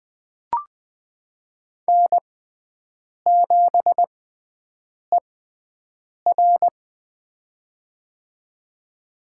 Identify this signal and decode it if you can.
Morse code, heard as a keyed tone, about 20 wpm, spaced as for 7 wpm, 700 Hz. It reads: N7ER
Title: USA Call Signs - 4 Characters - 6x Character Spacing - 20wpm